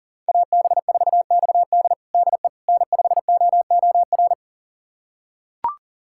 Morse code, heard as a keyed tone, about 37 wpm, 700 Hz. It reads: AB4XD DE N5OOR
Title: USA Callsign DE Callsign - 5 Characters - 40wpm